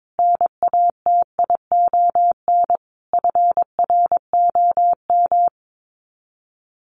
Morse code, heard as a keyed tone, about 22 wpm, 700 Hz. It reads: NATION FROM
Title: Sets of 5 Words 22wpm